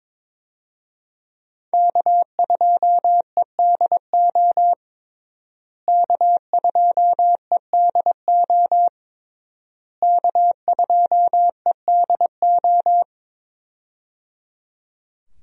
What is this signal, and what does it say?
Morse code, heard as a keyed tone, about 22 wpm, 700 Hz. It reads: K2EDO K2EDO K2EDO
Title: USA Call Signs - 5 Characters - Repeated 3x - 22wpm